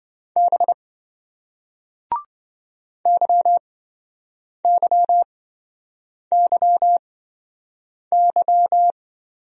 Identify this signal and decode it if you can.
Morse code, heard as a keyed tone, about 28 wpm, spaced as for 7 wpm, 700 Hz. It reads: B YYYY